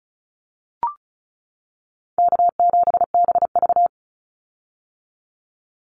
Morse code, heard as a keyed tone, about 36 wpm, 700 Hz. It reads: K7BV
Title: Call Signs - 35wpm